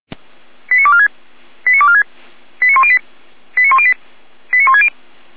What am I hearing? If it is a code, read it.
Five-tone selective call (ZVEI), heard as quick runs of five tones, tones of about 70 ms. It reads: 89246 89246 89108 89108 89160